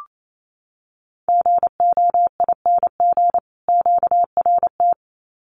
Morse code, heard as a keyed tone, about 28 wpm, 700 Hz. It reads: GOING QRT
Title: Common QSO Phrases 28wpm